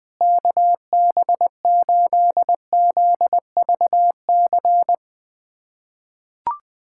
Morse code, heard as a keyed tone, about 20 wpm, 700 Hz. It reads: KB8ZVC
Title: USA Callsign DE Callsign - 6 Characters - 20wpm